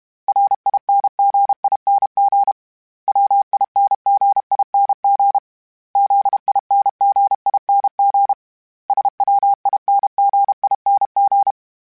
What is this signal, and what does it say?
Morse code, heard as a keyed tone, about 32 wpm, 800 Hz. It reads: RINGING WINGING ZINGING SWINGING
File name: Special INGs-mp3s / Special INGs-mp3s / Dble ings @30